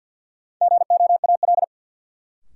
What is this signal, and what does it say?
Morse code, heard as a keyed tone, about 52 wpm, 700 Hz. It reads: GOAL
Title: Sentences from Top 1400 Words - New Word Drill Alt - 50wpm